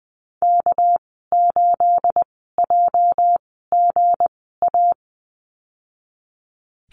Morse code, heard as a keyed tone, about 20 wpm, spaced as for 14 wpm, 700 Hz. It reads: K8JGA
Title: 245 Lesson 39g - Letter z - Call Signs 20wpm